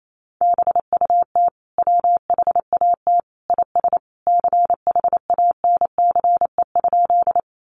Morse code, heard as a keyed tone, about 28 wpm, 700 Hz. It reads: BUT WHAT IS CHANCE?
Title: Sentences from Top 800 Words 28wpm